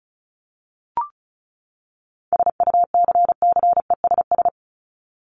Morse code, heard as a keyed tone, about 34 wpm, 700 Hz. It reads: SUCCESS
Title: Top 751-1000 Words 35wpm